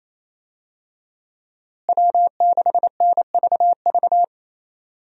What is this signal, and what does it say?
Morse code, heard as a keyed tone, about 28 wpm, 700 Hz. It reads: W6NVV
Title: USA Call Signs - 5 Characters - Repeated 3x - 28wpm